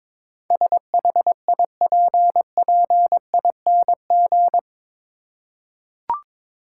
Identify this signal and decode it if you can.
Morse code, heard as a keyed tone, about 22 wpm, 700 Hz. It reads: SHIPPING
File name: practice / 22wpm / Eight Letter Words - 22wpm